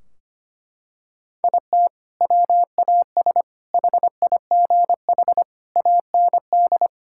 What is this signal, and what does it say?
Morse code, heard as a keyed tone, about 25 wpm, 700 Hz. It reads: IT WAS HIGH AND
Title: Sentences from Top 1600 Words 25wpm